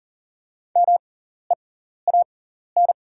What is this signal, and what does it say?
Morse code, heard as a keyed tone, about 37 wpm, spaced as for 14 wpm, 700 Hz. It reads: MEAN
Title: 6x Character Spacing - Top 200 Words - 40wpm